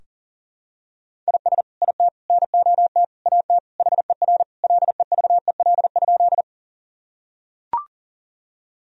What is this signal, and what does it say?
Morse code, heard as a keyed tone, about 40 wpm, 700 Hz. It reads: IS IT NOT AT HER LEVEL?
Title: Sentences from Top 1000 Words - Review Part 4 - 40wpm